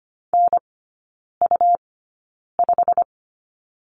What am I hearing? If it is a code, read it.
Morse code, heard as a keyed tone, about 25 wpm, spaced as for 9 wpm, 700 Hz. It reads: NU5
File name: Lesson 17e - Letter u - Three Characters 25wpm